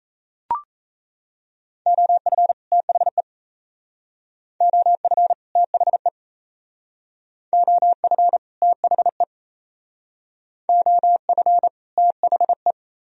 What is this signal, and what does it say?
Morse code, heard as a keyed tone, about 35 wpm, 700 Hz. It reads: OF THE OF THE OF THE OF THE